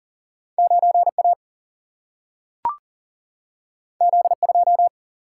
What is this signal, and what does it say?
Morse code, heard as a keyed tone, about 40 wpm, 700 Hz. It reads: 9A Z2
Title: Single-Letter Single-Number 40wpm